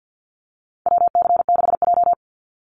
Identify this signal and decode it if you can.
Morse code, heard as a keyed tone, about 48 wpm, 700 Hz. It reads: WC6J
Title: NA SOTA Activator Callsigns 50wpm